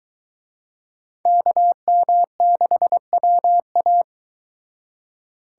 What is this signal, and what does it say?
Morse code, heard as a keyed tone, about 23 wpm, 700 Hz. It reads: KM6WA